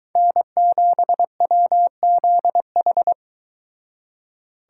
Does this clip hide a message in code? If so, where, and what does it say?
Yes: Morse code from 0.2 to 3.1 s, heard as a keyed tone, about 22 wpm, 700 Hz. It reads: N7WZH